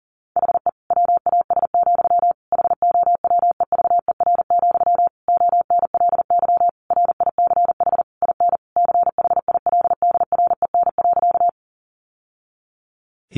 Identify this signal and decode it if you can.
Morse code, heard as a keyed tone, about 40 wpm, 700 Hz. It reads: HE WAS, HOWEVER, ONLY RICH IN CHILDREN.
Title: Sentences from Top 1100 Words 40wpm